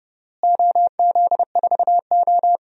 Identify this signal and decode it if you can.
Morse code, heard as a keyed tone, about 30 wpm, 700 Hz. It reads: OZ4O
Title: DXCC-100 Call Signs - 4 Characters - Repeated 2x - 30wpm